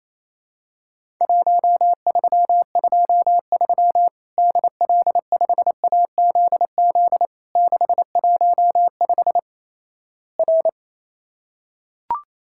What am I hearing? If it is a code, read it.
Morse code, heard as a keyed tone, about 28 wpm, 700 Hz. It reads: 1323 DL5AZZ 615 R
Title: Worked All Europe DX Contest - QTC Exchange - 28wpm